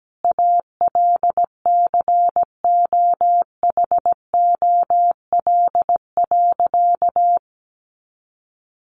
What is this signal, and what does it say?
Morse code, heard as a keyed tone, about 17 wpm, 700 Hz. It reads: ALCOHOL.
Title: Sentences from Top 1400 Words 17wpm